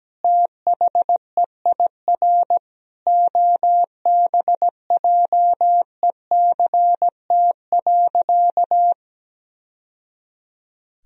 Morse code, heard as a keyed tone, about 17 wpm, 700 Hz. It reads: THEIR OBJECT.
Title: Sentences from Top 1000 Words - Review Part 3 - 17wpm